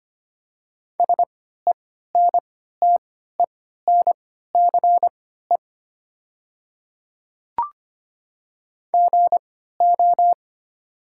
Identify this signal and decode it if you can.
Morse code, heard as a keyed tone, about 25 wpm, spaced as for 14 wpm, 700 Hz. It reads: SENTENCE GO
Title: 3x Character Spacing - Top 200 Words - 25wpm